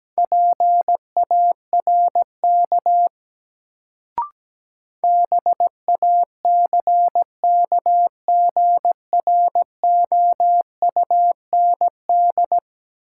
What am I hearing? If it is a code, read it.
Morse code, heard as a keyed tone, about 17 wpm, 700 Hz. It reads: PARK BACKGROUND
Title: Sentences from Top 1300 Words - New Word Drill - 17wpm